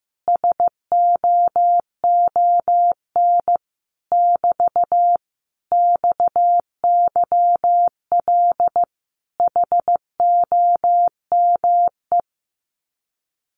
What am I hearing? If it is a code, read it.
Morse code, heard as a keyed tone, about 15 wpm, 700 Hz. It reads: SOON <BT> XYL HOME
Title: Common QSO Phrases Encore 15wpm